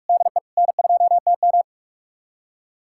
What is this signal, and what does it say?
Morse code, heard as a keyed tone, about 47 wpm, 700 Hz. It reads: DE N2TM
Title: USA Callsign DE Callsign - 4 Characters - 45wpm